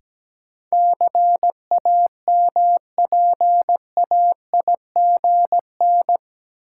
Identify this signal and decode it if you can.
Morse code, heard as a keyed tone, about 17 wpm, 700 Hz. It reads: CAMPAIGN
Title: Sentences from Top 1200 Words - New Word Drill Alt - 17wpm